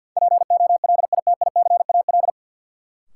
Morse code, heard as a keyed tone, about 47 wpm, 700 Hz. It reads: POLITICAL
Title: Sentences from Top 800 Words - New Word Drill - 50wpm